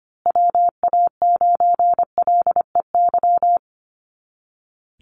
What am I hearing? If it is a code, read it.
Morse code, heard as a keyed tone, about 25 wpm, 700 Hz. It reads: WA9LEY
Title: Call Signs Repeated 2x - 25wpm